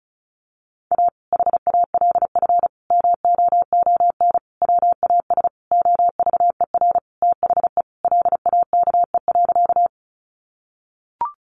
Morse code, heard as a keyed tone, about 34 wpm, 700 Hz. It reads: A HALF MOON WAS OVER THE LAKE.
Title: Sentences from Top 1100 Words 35wpm